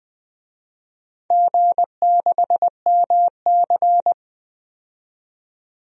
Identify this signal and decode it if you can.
Morse code, heard as a keyed tone, about 20 wpm, 700 Hz. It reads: G6MC
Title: DXCC-100 Call Signs - 4 Characters - 20wpm